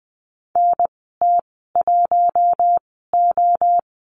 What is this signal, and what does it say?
Morse code, heard as a keyed tone, about 20 wpm, spaced as for 14 wpm, 700 Hz. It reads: NT1O
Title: 042 Lesson 8g - Number 1 - Call Signs 20wpm